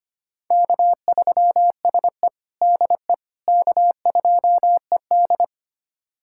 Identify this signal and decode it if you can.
Morse code, heard as a keyed tone, about 25 wpm, 700 Hz. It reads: K3SE DE K2ED
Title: USA Callsign DE Callsign - 4 Characters - 25wpm